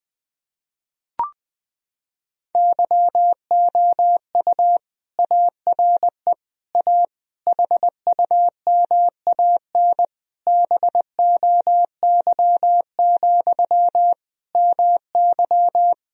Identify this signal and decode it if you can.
Morse code, heard as a keyed tone, about 21 wpm, 700 Hz. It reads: YOU ARE A HUMAN BOY, MY
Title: Sentences from Top 1000 Words - Review Part 3 - 20wpm